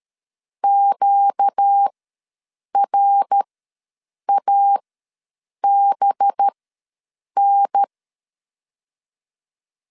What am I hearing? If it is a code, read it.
Morse code, heard as a keyed tone, about 13 wpm, spaced as for 7 wpm, 800 Hz. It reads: QRABN
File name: LEZIONI / 5 LBVC / 5 LB ws 12wpm 3034